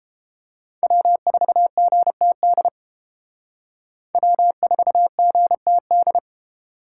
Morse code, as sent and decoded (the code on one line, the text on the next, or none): .-- ....- --. - -.. / .-- ....- --. - -..
W4GTD W4GTD